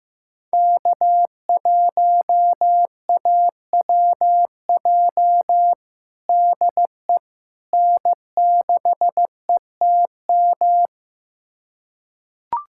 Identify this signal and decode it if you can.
Morse code, heard as a keyed tone, about 15 wpm, 700 Hz. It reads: K1AWJ DE N6ETM
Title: USA Callsign DE Callsign 15wpm